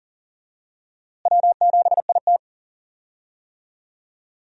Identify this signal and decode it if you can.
Morse code, heard as a keyed tone, about 37 wpm, 700 Hz. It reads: W7IT
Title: USA Call Signs - 4 Characters - Repeated 3x - 40wpm